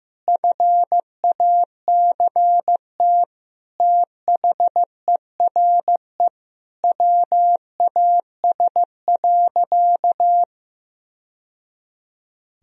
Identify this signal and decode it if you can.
Morse code, heard as a keyed tone, about 15 wpm, 700 Hz. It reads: FACT THERE WAS.
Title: Sentences from Top 1000 Words - Review Part 3 - 15wpm